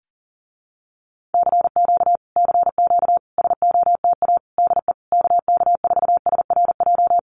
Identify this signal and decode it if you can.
Morse code, heard as a keyed tone, about 40 wpm, 700 Hz. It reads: CQ CQ SOTA DE KK4SRJ
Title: Common QSO Phrases Encore 40wpm